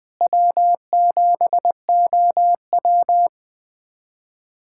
Morse code, heard as a keyed tone, about 20 wpm, 700 Hz. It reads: W7OW
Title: USA Callsign DE Callsign - 4 Characters - 20wpm